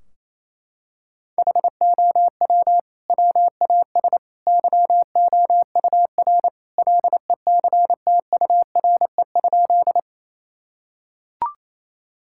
Morse code, heard as a keyed tone, about 28 wpm, 700 Hz. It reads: HOW WAS YOUR LECTURE?
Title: Sentences from Top 1600 Words 28wpm